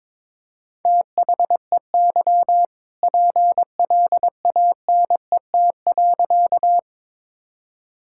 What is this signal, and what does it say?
Morse code, heard as a keyed tone, about 23 wpm, 700 Hz. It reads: THEY PLANET.
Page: morse-code-ninja-practice-22wpm